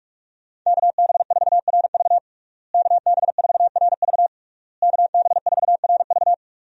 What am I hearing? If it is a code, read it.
Morse code, heard as a keyed tone, about 46 wpm, 700 Hz. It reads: KB4RV KB4RV KB4RV
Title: USA Call Signs - 5 Characters - Repeated 3x - 45wpm